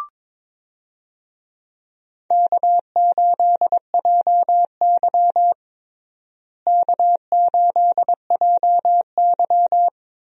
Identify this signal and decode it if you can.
Morse code, heard as a keyed tone, about 22 wpm, 700 Hz. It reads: K8JY K8JY
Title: USA Call Signs - 4 Characters - Repeated 2x - 22wpm